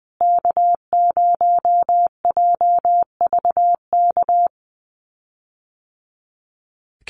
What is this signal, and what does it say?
Morse code, heard as a keyed tone, about 20 wpm, 700 Hz. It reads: K0JVK
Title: USA Call Signs Repeated 3x 20wpm